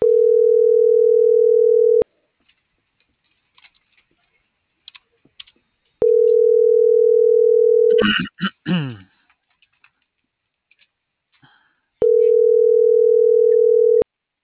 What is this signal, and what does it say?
Ringback tone (North America), 440 + 480 Hz.